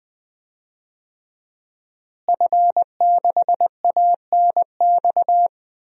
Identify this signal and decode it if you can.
Morse code, heard as a keyed tone, about 20 wpm, 700 Hz. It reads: F6ANX
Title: DXCC-100 Call Signs - 5 Characters - Repeated 2x - 20wpm